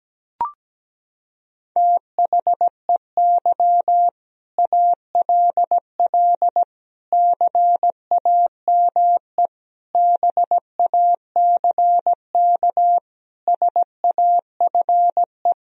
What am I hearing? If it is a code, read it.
Morse code, heard as a keyed tone, about 17 wpm, 700 Hz. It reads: THEY ALL CAME BACK SAFE
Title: Sentences from Top 1000 Words - Review Part 1 - 17wpm